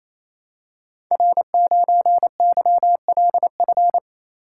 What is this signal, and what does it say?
Morse code, heard as a keyed tone, about 27 wpm, 700 Hz. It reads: R9YLF
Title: Code Groups - 5 Letter:Numbers - 28wpm